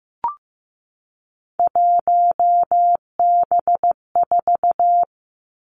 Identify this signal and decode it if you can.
Morse code, heard as a keyed tone, about 15 wpm, 700 Hz. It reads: 1B4
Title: Three Challenging Letter-Number Combinations 15wpm